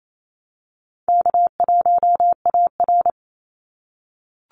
Morse code, heard as a keyed tone, about 28 wpm, 700 Hz. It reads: K1AR
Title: Easy Call Signs - 28wpm